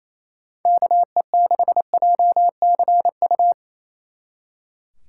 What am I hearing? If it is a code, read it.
Morse code, heard as a keyed tone, about 34 wpm, 700 Hz. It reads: KE6JCU
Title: Speed Racing - USA 2x3 Call Signs 28wpm